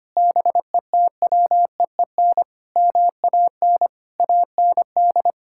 Morse code, heard as a keyed tone, about 25 wpm, 700 Hz. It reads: BETWEEN MAN AND
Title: Sentences from Top 1600 Words 25wpm